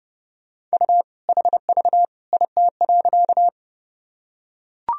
Morse code, heard as a keyed tone, about 30 wpm, 700 Hz. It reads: U HV IT.
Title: Sentences from Top 100 Words and Abbreviations 30wpm